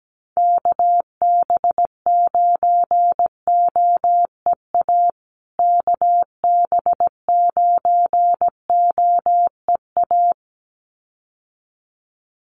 Morse code, heard as a keyed tone, about 17 wpm, 700 Hz. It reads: KB9OEA KB9OEA
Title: USA Call Signs Repeated 2x 17wpm